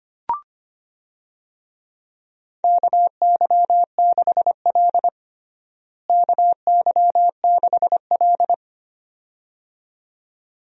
Morse code, heard as a keyed tone, about 25 wpm, 700 Hz. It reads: KY6L KY6L
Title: USA Call Signs - 4 Characters - Repeated 2x - 25wpm